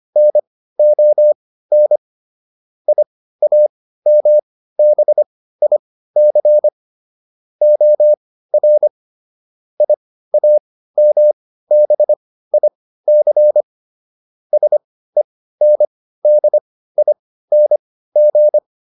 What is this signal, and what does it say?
Morse code, heard as a keyed tone, about 25 wpm, spaced as for 15 wpm, 600 Hz. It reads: NON IAMBIC OR IAMBIC SENDING